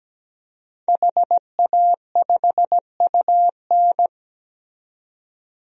Morse code, heard as a keyed tone, about 17 wpm, 700 Hz. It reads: HA5UN
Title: DXCC-100 Call Signs - 5 Characters - 17wpm